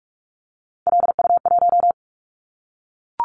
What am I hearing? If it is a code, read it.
Morse code, heard as a keyed tone, about 44 wpm, 700 Hz. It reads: LU1
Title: Callsign Prefixes for Top 100 DXCC - 45wpm